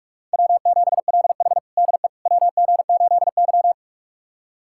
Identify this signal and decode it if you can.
Morse code, heard as a keyed tone, about 46 wpm, 700 Hz. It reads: W7LH DE WG8Y
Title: USA Callsign DE Callsign - 4 Characters - 45wpm